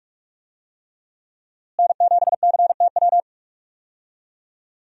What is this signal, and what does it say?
Morse code, heard as a keyed tone, about 46 wpm, 700 Hz. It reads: N7CTW
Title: USA Call Signs - 5 Characters - Repeated 3x - 45wpm